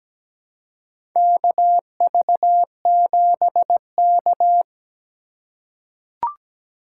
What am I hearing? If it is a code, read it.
Morse code, heard as a keyed tone, about 17 wpm, 700 Hz. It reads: KV7K
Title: USA Call Signs - 4 Characters - 4x Character Spacing - 17wpm